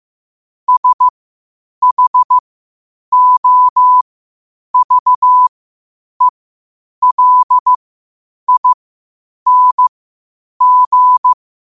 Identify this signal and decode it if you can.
Morse code, heard as a keyed tone, about 15 wpm, spaced as for 8 wpm, 1000 Hz. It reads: SHOVELING
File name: qotdfiles / 8wpm